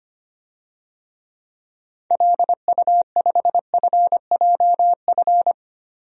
Morse code, heard as a keyed tone, about 25 wpm, 700 Hz. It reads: LU5FJF
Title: DXCC-100 Call Signs - 6 Characters - 25wpm